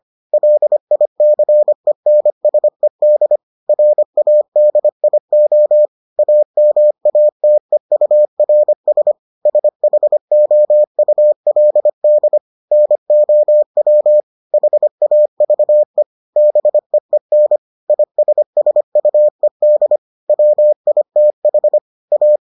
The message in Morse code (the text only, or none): LICENSED RADIO AMATEURS SHOULD NOW HAVE BEEN ISSUED WITH A